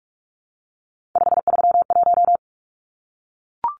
Morse code, heard as a keyed tone, about 44 wpm, 700 Hz. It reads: 531
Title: Numbers - 3-Digit 45wpm